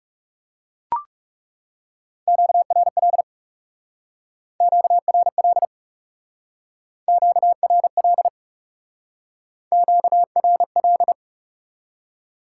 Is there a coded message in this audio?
Yes: Morse code from 2.3 to 11.1 s, heard as a keyed tone, about 38 wpm, 700 Hz. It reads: QRL QRL QRL QRL